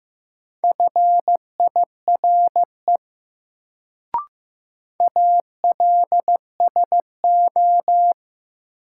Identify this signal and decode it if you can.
Morse code, heard as a keyed tone, about 15 wpm, 700 Hz. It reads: FIRE ALSO
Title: Sentences from Top 400 Words - New Word Drill - 15wpm